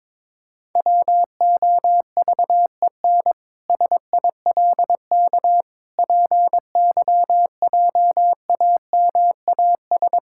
Morse code, heard as a keyed tone, about 22 wpm, 700 Hz. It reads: WOVEN SILK PYJAMAS
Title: Pangram Challenge v2 - 22wpm